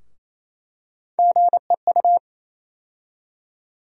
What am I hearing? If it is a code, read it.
Morse code, heard as a keyed tone, about 28 wpm, 700 Hz. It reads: GEU
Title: USA Call Sign Postfixes - 3 Letters 28wpm